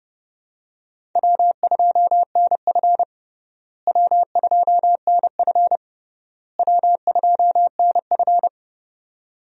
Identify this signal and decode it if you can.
Morse code, heard as a keyed tone, about 30 wpm, 700 Hz. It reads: W2NF W2NF W2NF
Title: USA Call Signs - 4 Characters - Repeated 3x - 30wpm